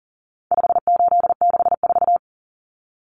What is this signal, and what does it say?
Morse code, heard as a keyed tone, about 37 wpm, 700 Hz. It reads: 5864
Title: Numbers - 4-Digit 40wpm